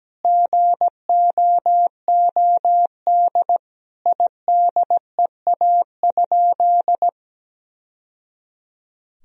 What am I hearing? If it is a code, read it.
Morse code, heard as a keyed tone, about 17 wpm, 700 Hz. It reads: GOOD IDEA?
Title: Sentences from Top 1000 Words - Review Part 4 - 17wpm